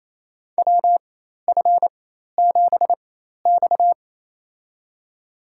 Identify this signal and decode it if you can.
Morse code, heard as a keyed tone, about 28 wpm, spaced as for 13 wpm, 700 Hz. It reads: WF7X
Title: USA Call Signs - 4 Characters - 4x Character Spacing - 28wpm